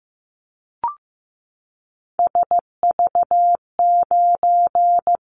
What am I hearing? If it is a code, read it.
Morse code, heard as a keyed tone, about 15 wpm, 700 Hz. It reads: SV9
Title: Callsign Prefixes for Top 100 DXCC - 15wpm